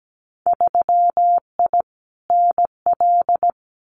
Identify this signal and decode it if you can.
Morse code, heard as a keyed tone, about 17 wpm, 700 Hz. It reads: 3I NL
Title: Winter Field Day Exchanges 17wpm